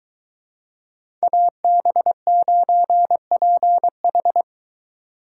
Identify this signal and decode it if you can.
Morse code, heard as a keyed tone, about 23 wpm, 700 Hz. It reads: AB9PH